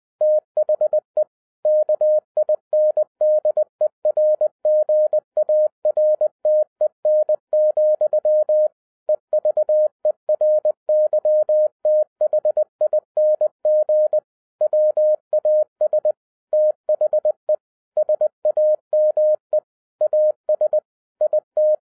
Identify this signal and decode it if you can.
Morse code, heard as a keyed tone, about 20 wpm, 610 Hz. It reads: THE KINDERGARTEN, EVERYTHING WAS THE SAME AS IT
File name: 2011-10-mp3 / n03-TARO JR0QWW 20wpm